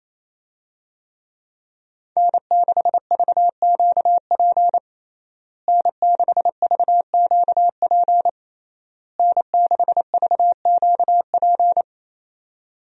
Morse code, heard as a keyed tone, about 28 wpm, 700 Hz. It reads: N6VQP N6VQP N6VQP
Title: USA Call Signs - 5 Characters - Repeated 3x - 28wpm